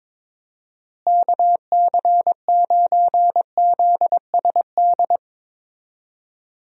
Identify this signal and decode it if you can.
Morse code, heard as a keyed tone, about 22 wpm, 700 Hz. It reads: KC9ZSD
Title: USA Call Signs - 6 Characters - Repeated 3x - 22wpm